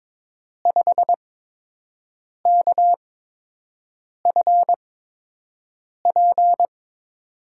Morse code, heard as a keyed tone, about 22 wpm, spaced as for 6 wpm, 700 Hz. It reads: 5KFP
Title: USA Call Signs - 5 Characters - 8x Character Spacing - 22wpm